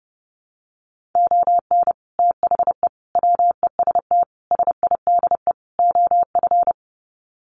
Morse code, heard as a keyed tone, about 30 wpm, 700 Hz. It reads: ON THE WEST SIDE OF
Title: Sets of 5 Words Encore2 30wpm